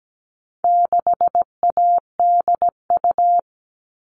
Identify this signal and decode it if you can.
Morse code, heard as a keyed tone, about 17 wpm, 700 Hz. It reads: 6ADU
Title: All Cut Numbers - 4-Digit 17wpm